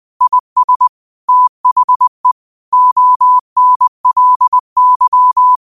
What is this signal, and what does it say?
Morse code, heard as a keyed tone, about 20 wpm, 1000 Hz. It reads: IS THE ONLY